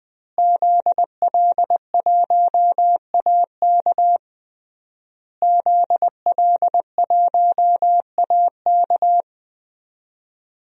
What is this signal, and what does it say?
Morse code, heard as a keyed tone, about 20 wpm, 700 Hz. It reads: ZL1AK ZL1AK
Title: DXCC-100 Call Signs - 5 Characters - Repeated 3x - 20wpm